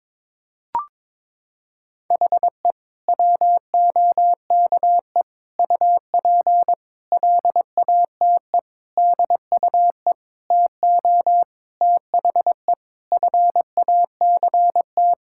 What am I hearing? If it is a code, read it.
Morse code, heard as a keyed tone, about 22 wpm, 700 Hz. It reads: HE WOKE UP LATE DUE TO THE FACT
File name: practice / 22wpm / Sentences with 4-Word Phrases - 22wpm